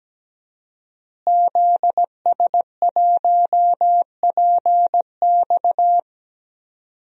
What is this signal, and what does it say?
Morse code, heard as a keyed tone, about 17 wpm, 700 Hz. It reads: ZS1PX
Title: DXCC-100 Call Signs - 5 Characters - Repeated 2x - 17wpm